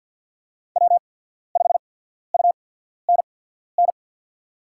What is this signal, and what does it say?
Morse code, heard as a keyed tone, about 53 wpm, spaced as for 14 wpm, 700 Hz. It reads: W5UNN
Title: USA Call Signs - 5 Characters - 8x Character Spacing - 50wpm